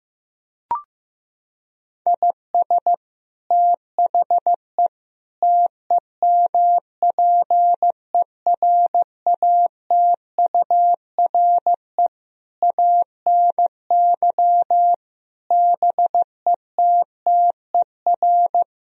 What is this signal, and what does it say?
Morse code, heard as a keyed tone, about 16 wpm, 700 Hz. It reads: IS THE TEMPERATURE ANY BETTER
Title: Sentences from Top 1000 Words - Review Part 4 - 15wpm